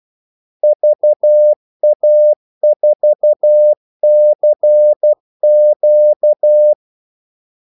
Morse code, heard as a keyed tone, about 12 wpm, 600 Hz. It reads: VA4CQ
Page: 2 ekstra ordmellemrum mellem kaldesignalerne